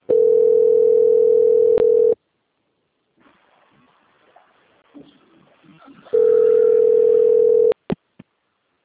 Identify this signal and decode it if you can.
Ringback tone (North America), 440 + 480 Hz.
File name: out- - - - - .3224